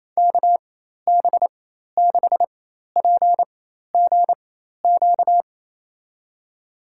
Morse code, heard as a keyed tone, about 27 wpm, spaced as for 13 wpm, 700 Hz. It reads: KB6PGQ
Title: USA Call Signs - 4x Character Spacing - 28wpm